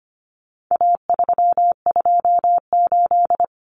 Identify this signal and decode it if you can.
Morse code, heard as a keyed tone, about 25 wpm, 700 Hz. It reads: A328
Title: Common Cut Numbers - 4-Digit 25wpm